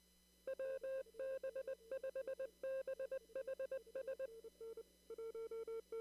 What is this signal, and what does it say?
Morse code, heard as a keyed tone, about 20 wpm, 530 Hz. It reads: WB5BHS
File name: CW / kt5j wpx cw-op-k5pi